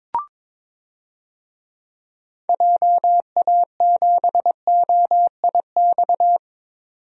Morse code, heard as a keyed tone, about 22 wpm, 700 Hz. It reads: JA7OIX
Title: DXCC-100 Call Signs - 6 Characters - 22wpm